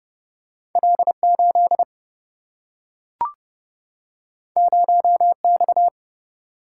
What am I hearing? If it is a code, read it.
Morse code, heard as a keyed tone, about 30 wpm, 700 Hz. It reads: L8 0X
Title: Single-Letter Single-Number 30wpm